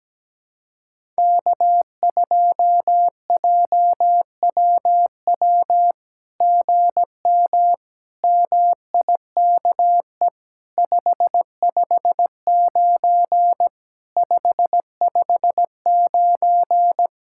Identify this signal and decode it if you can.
Morse code, heard as a keyed tone, about 17 wpm, 700 Hz. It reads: K2JWW GM MIKE 559 559
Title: SOTA QSO Practice 17wpm